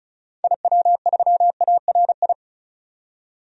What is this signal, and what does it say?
Morse code, heard as a keyed tone, about 36 wpm, 700 Hz. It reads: IW3ARI
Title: DXCC-100 Call Signs - 6 Characters - 35wpm